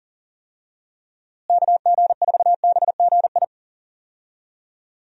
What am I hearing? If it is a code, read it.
Morse code, heard as a keyed tone, about 40 wpm, 700 Hz. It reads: KG4BGI
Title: USA Call Signs - 6 Characters - Repeated 2x - 40wpm